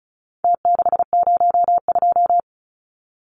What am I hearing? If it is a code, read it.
Morse code, heard as a keyed tone, about 34 wpm, 700 Hz. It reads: T602
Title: Numbers - Contest Style - 4-Digit 35wpm